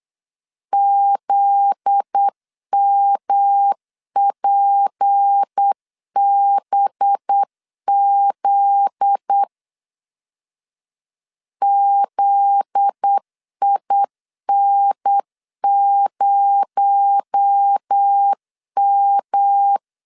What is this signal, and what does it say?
Morse code, heard as a keyed tone, about 8 wpm, 800 Hz. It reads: ZMPBZ ZIN0M
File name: LEZIONI / 7 GF9Z / 7 GF9Z 8wpm 3213